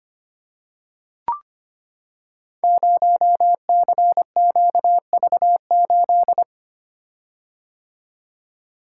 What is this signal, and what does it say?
Morse code, heard as a keyed tone, about 25 wpm, 700 Hz. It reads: 0CQV8
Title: Code Groups - 5 Letter:Numbers - 25wpm